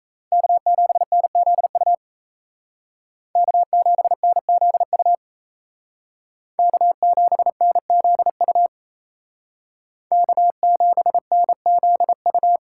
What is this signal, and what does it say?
Morse code, heard as a keyed tone, about 36 wpm, 700 Hz. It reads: K7NZU K7NZU K7NZU K7NZU